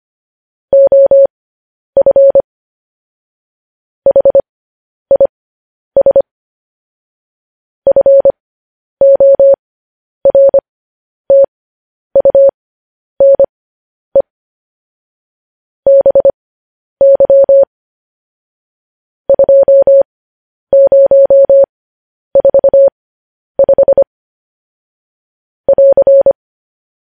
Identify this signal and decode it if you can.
Morse code, heard as a keyed tone, about 25 wpm, spaced as for 10 wpm, 550 Hz. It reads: OF HIS FORTUNE BY 2045 <AR>
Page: News Headlines in Morse Code at 10 WPM